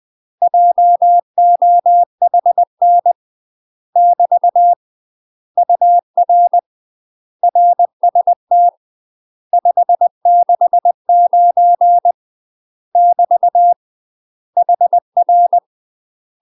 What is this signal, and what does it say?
Morse code, heard as a keyed tone, about 20 wpm, 700 Hz. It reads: JOHN <BT> UR RST 569 <BT> HR
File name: UK MASTER QSOs TRAINING/20WPM